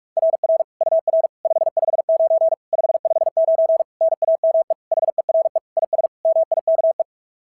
Morse code, heard as a keyed tone, about 48 wpm, 660 Hz. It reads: RR UR 559 559 NAME HERE IS MIKE
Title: SOTA QSO Practice 45wpm